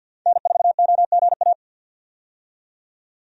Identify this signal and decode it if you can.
Morse code, heard as a keyed tone, about 51 wpm, 700 Hz. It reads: N4OGA
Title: USA Call Signs - 5 Characters - 50wpm